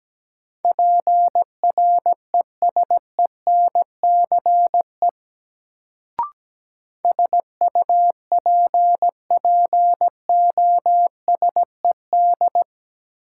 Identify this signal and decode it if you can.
Morse code, heard as a keyed tone, about 17 wpm, 700 Hz. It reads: PRESENCE SUPPOSED
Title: Sets of 3 Words - New Word Drill - 17wpm